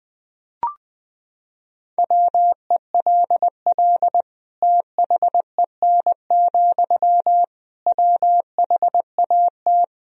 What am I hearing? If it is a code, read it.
Morse code, heard as a keyed tone, about 21 wpm, 700 Hz. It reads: WELL THEN, WHAT
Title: Sentences from Top 200 Words and Abbreviations 20wpm